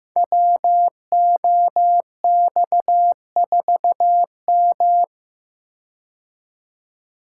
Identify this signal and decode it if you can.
Morse code, heard as a keyed tone, about 15 wpm, 700 Hz. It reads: WOX4M
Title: Code Groups - 5 Letter:Numbers - 15wpm